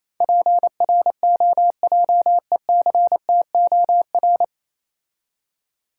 Morse code, heard as a keyed tone, about 28 wpm, 690 Hz. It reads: PROJECTOR